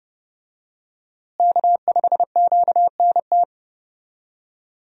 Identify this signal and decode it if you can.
Morse code, heard as a keyed tone, about 30 wpm, 700 Hz. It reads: K5QNT
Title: USA Call Signs - 5 Characters - Repeated 2x - 30wpm